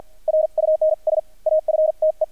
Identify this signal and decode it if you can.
Morse code, heard as a keyed tone, about 48 wpm, spaced as for 42 wpm, 620 Hz. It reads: RUNS AFTE
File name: archived / CRSnet